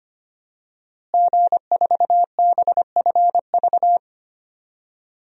Morse code, heard as a keyed tone, about 25 wpm, 700 Hz. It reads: G4BFV
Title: DXCC-100 Call Signs - 5 Characters - Repeated 2x - 25wpm